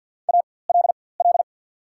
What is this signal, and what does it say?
Morse code, heard as a keyed tone, about 54 wpm, spaced as for 24 wpm, 690 Hz. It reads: ALL
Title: 4x Character Spacing - Top 100 Words - 50wpm